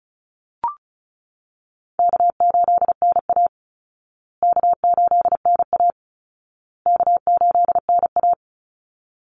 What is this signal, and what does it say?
Morse code, heard as a keyed tone, about 34 wpm, 700 Hz. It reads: K8NA K8NA K8NA
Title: Easy Call Signs Repeated 3x - 35wpm